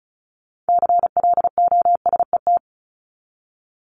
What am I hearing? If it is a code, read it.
Morse code, heard as a keyed tone, about 34 wpm, 700 Hz. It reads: CLOSET
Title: Top 1751-2000 Words 35wpm